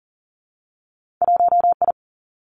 Morse code, heard as a keyed tone, about 42 wpm, 700 Hz. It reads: 1I
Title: Two Challenging Letter-Number Combinations 40wpm